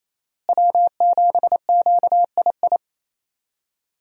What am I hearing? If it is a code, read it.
Morse code, heard as a keyed tone, about 28 wpm, 700 Hz. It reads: W7QII
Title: USA Call Signs - 5 Characters - Repeated 2x - 28wpm